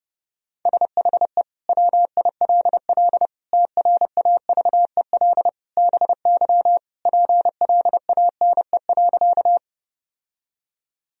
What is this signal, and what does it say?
Morse code, heard as a keyed tone, about 30 wpm, 700 Hz. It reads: SHE WILL TRAVEL BY PLANE.
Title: Sentences from Top 1000 Words - Review Part 3 - 30wpm